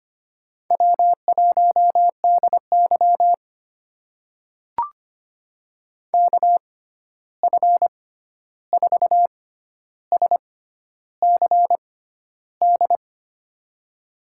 Morse code, heard as a keyed tone, about 25 wpm, spaced as for 11 wpm, 700 Hz. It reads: W1DY KF4SCD
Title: USA Call Signs - 6x Character Spacing - 25wpm